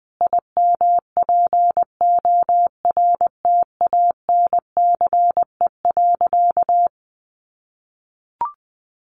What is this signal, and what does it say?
Morse code, heard as a keyed tone, about 20 wpm, 700 Hz. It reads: IMPORTANCE.
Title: Sentences from Top 1200 Words 20wpm